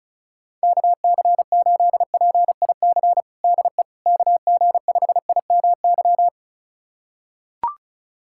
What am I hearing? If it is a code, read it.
Morse code, heard as a keyed tone, about 34 wpm, 700 Hz. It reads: KC8PIC DE KG5IMY
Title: USA Callsign DE Callsign - 6 Characters - 35wpm